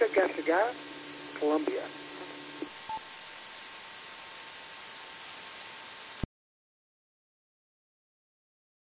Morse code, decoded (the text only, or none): none